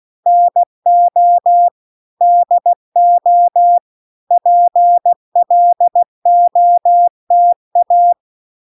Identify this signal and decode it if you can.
Morse code, heard as a keyed tone, about 16 wpm, 700 Hz. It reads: NO DO PLOTA